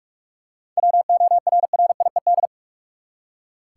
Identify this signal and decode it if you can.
Morse code, heard as a keyed tone, about 46 wpm, 700 Hz. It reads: WORRIED
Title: Seven Letter Words - 45wpm